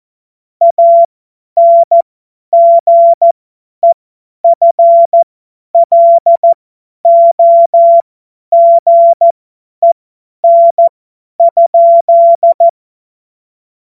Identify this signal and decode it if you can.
Morse code, heard as a keyed tone, about 14 wpm, spaced as for 10 wpm, 680 Hz. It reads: ANGEFLOGEN?